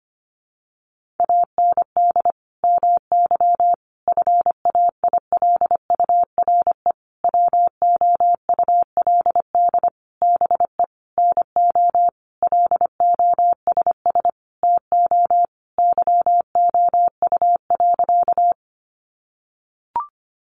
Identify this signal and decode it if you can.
Morse code, heard as a keyed tone, about 25 wpm, 700 Hz. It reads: AND MY FAILURE WOULD BE NO LOSS TO YOU.
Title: Sentences from Top 1400 Words 25wpm